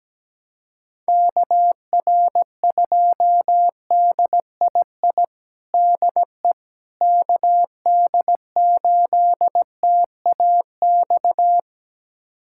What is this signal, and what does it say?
Morse code, heard as a keyed tone, about 17 wpm, 700 Hz. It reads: KR2DII DE KD8TAX
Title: USA Callsign DE Callsign 17wpm